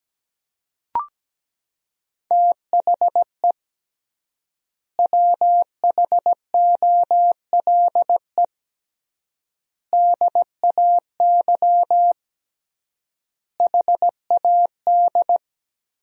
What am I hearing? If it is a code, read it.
Morse code, heard as a keyed tone, about 17 wpm, 700 Hz. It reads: THE WHOLE DAY HAD
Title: 3x Word-Spacing - Sentences from Top 500 Words - 17wpm